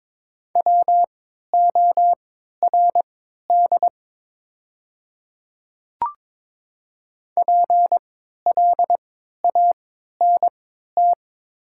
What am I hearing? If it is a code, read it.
Morse code, heard as a keyed tone, about 22 wpm, spaced as for 12 wpm, 700 Hz. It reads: WORD PLANT
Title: 3x Character Spacing - Top 300 Words - 22wpm